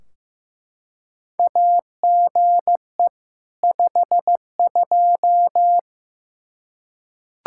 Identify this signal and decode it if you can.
Morse code, heard as a keyed tone, about 15 wpm, 700 Hz. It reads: AGE 52
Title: Practice QSO - Exchange 3 - More Info - 15wpm